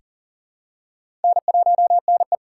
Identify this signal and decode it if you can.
Morse code, heard as a keyed tone, about 40 wpm, 700 Hz. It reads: N1NE
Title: USA Call Signs - 6x Character Spacing - 40wpm